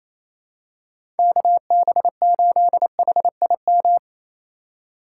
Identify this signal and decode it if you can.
Morse code, heard as a keyed tone, about 28 wpm, 700 Hz. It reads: KB8HIM
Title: USA Call Signs - 6 Characters - 4x Character Spacing - 28wpm